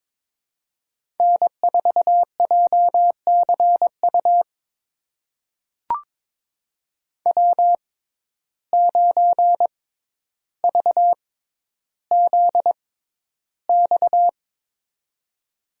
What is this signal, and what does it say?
Morse code, heard as a keyed tone, about 22 wpm, spaced as for 11 wpm, 700 Hz. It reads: N4JCU W9VZX
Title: USA Call Signs - 5 Characters - 6x Character Spacing - 22wpm